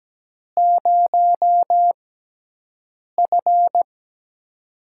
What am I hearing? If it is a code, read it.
Morse code, heard as a keyed tone, about 17 wpm, spaced as for 6 wpm, 700 Hz. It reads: 0F
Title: USA Call Signs - 6x Character Spacing - 17wpm